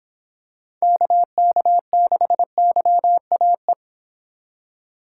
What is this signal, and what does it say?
Morse code, heard as a keyed tone, about 25 wpm, 700 Hz. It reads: KK6YAE